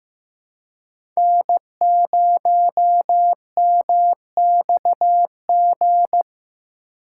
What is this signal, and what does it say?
Morse code, heard as a keyed tone, about 15 wpm, 700 Hz. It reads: N0MXG
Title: USA Call Signs - 5 Characters - 2x Character Spacing - 15wpm